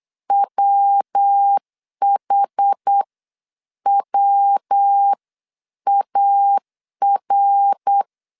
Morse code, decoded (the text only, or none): WHWAR